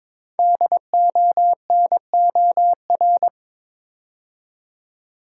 Morse code, heard as a keyed tone, about 23 wpm, 690 Hz. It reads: DONOR
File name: 5 Letter Words - First 3 Letters Given - 22wpm